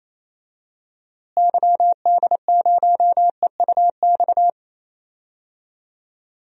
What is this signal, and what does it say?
Morse code, heard as a keyed tone, about 28 wpm, 700 Hz. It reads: YD0EUX
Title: DXCC-100 Call Signs - 6 Characters - Repeated 3x - 28wpm